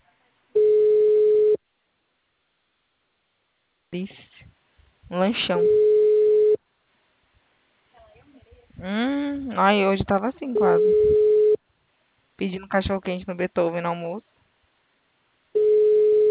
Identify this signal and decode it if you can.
Ringback tone (Europe, 425 Hz), with speech.